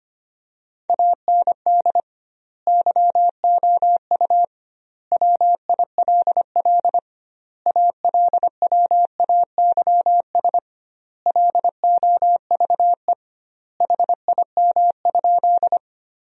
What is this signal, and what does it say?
Morse code, heard as a keyed tone, about 25 wpm, 700 Hz. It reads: AND YOU WILL ALWAYS LOVE HIM?
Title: 2x Word-Spacing - Sentences from Top 400 Words - 25wpm